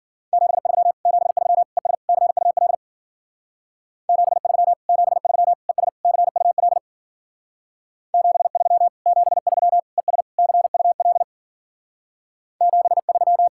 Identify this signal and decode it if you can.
Morse code, heard as a keyed tone, about 51 wpm, 700 Hz. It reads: 73 73 ES CUL 73 73 ES CUL 73 73 ES CUL 73